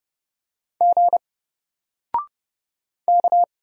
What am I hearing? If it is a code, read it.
Morse code, heard as a keyed tone, about 30 wpm, 700 Hz. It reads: G K
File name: Single-Letter-Number-30wpm